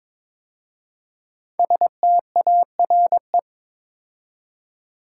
Morse code, heard as a keyed tone, about 22 wpm, 700 Hz. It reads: STARE
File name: 5 Letter Words - First 3 Letters Given - 22wpm